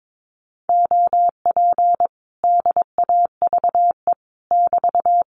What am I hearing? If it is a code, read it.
Morse code, heard as a keyed tone, about 22 wpm, 700 Hz. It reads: OP DAVE <BT>
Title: Common QSO Phrases Encore 22wpm